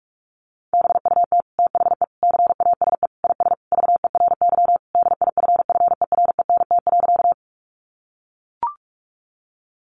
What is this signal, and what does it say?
Morse code, heard as a keyed tone, about 44 wpm, 700 Hz. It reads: BUT THE CASE IS VERY DIFFERENT.
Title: Sentences from Top 700 Words 45wpm